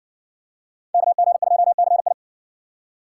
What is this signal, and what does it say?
Morse code, heard as a keyed tone, about 62 wpm, 700 Hz. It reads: KG2CI